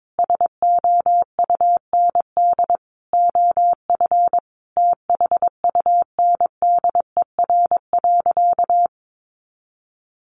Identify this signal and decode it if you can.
Morse code, heard as a keyed tone, about 22 wpm, 700 Hz. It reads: SOUND OF THUNDER.
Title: A Sound Of Thunder 22wpm